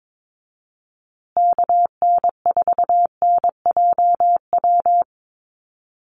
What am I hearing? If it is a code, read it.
Morse code, heard as a keyed tone, about 22 wpm, 700 Hz. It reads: KN4NJW
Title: USA Call Signs Repeated 2x 22wpm